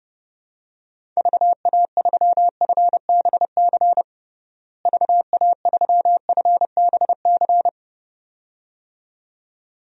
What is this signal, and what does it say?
Morse code, heard as a keyed tone, about 31 wpm, 700 Hz. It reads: VA3FBC VA3FBC
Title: DXCC-100 Call Signs - 6 Characters - Repeated 2x - 30wpm